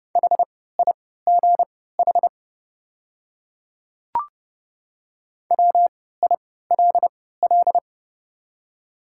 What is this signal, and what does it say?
Morse code, heard as a keyed tone, about 32 wpm, spaced as for 17 wpm, 700 Hz. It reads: HIGH WILL
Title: 3x Character Spacing - Top 200 Words - 30wpm